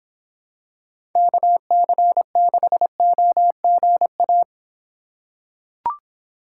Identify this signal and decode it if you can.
Morse code, heard as a keyed tone, about 26 wpm, 700 Hz. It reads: KC6OGA